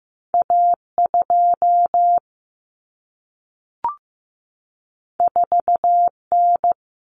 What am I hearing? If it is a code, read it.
Morse code, heard as a keyed tone, about 15 wpm, 700 Hz. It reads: A 2 4 N
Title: Common Cut Numbers - 2-Digit 15wpm